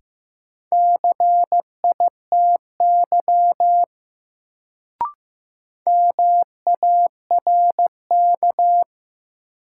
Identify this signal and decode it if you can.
Morse code, heard as a keyed tone, about 15 wpm, 700 Hz. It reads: CITY MARK
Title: Sentences from Top 300 Words - New Word Drill - 15wpm